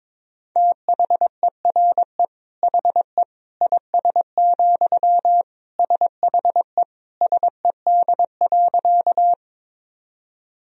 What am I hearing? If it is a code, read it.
Morse code, heard as a keyed tone, about 22 wpm, 700 Hz. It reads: THERE HE IS, SHE SED.
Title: Sentences from Top 100 Words and Abbreviations 22wpm